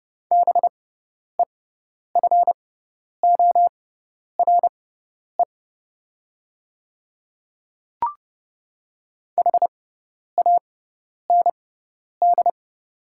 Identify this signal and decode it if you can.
Morse code, heard as a keyed tone, about 30 wpm, spaced as for 10 wpm, 700 Hz. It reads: BEFORE HAND
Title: 6x Character Spacing - Top 300 Words - 30wpm